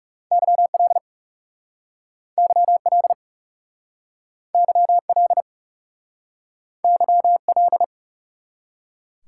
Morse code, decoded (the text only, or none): YL YL YL YL